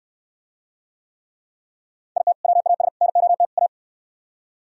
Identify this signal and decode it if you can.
Morse code, heard as a keyed tone, about 64 wpm, 700 Hz. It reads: IT WAS NOT A